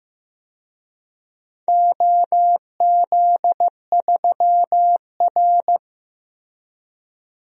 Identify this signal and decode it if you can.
Morse code, heard as a keyed tone, about 15 wpm, 700 Hz. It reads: OZ3R
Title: DXCC-100 Call Signs - 4 Characters - Repeated 2x - 15wpm